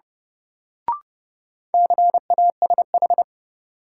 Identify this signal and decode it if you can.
Morse code, heard as a keyed tone, about 29 wpm, 700 Hz. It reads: CASH
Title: Sentences from Top 700 Words - New Word Drill - 30wpm